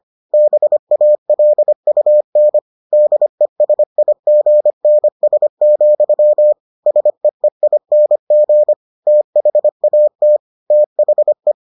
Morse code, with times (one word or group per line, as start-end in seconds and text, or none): 0.3-2.6 BALUN
2.9-6.5 DESIGNS,
6.9-8.7 SEEING
9.1-10.4 THAT
10.7-11.5 THE